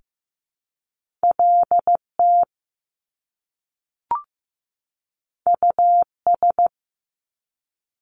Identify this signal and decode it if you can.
Morse code, heard as a keyed tone, about 15 wpm, 700 Hz. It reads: LT US
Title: Top Two Letter Combinations in Words 15wpm